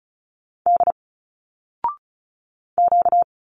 Morse code, heard as a keyed tone, about 35 wpm, 700 Hz. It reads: D Q
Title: Single Letters 35wpm